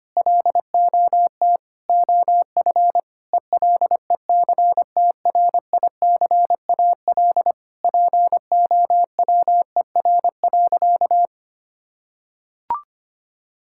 Morse code, heard as a keyed tone, about 25 wpm, 700 Hz. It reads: LOT OF ELECTRICAL POWER.
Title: Sentences from Top 1000 Words - Review Part 6 - 25wpm